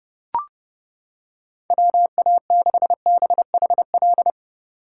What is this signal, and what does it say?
Morse code, heard as a keyed tone, about 29 wpm, 700 Hz. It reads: WA6BHL